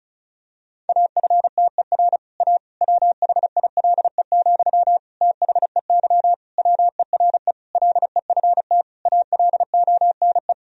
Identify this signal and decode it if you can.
Morse code, heard as a keyed tone, about 34 wpm, 700 Hz. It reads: AFTER A WHILE, THEY WERE LEFT ALONE
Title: Sentences from Top 1000 Words - Review Part 1 - 35wpm